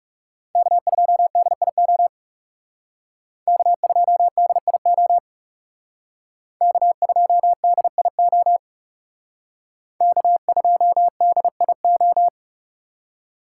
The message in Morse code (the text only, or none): K2DIO K2DIO K2DIO K2DIO